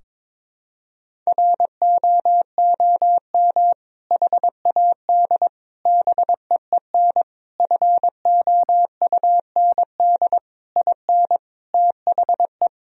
Morse code, heard as a keyed tone, about 22 wpm, 700 Hz. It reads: ROOM HAD BEEN FOUND IN THE
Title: Sentences from Top 1000 Words - Review Part 3 - 22wpm